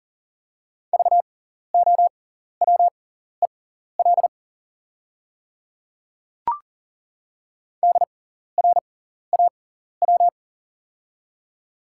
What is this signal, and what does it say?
Morse code, heard as a keyed tone, about 42 wpm, spaced as for 14 wpm, 690 Hz. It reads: VOWEL DRAW
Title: 6x Character Spacing - Top 500 Words - 40wpm